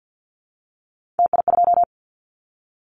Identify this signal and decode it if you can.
Morse code, heard as a keyed tone, about 50 wpm, 700 Hz. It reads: TI2
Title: Callsign Prefixes for Top 100 DXCC - 50wpm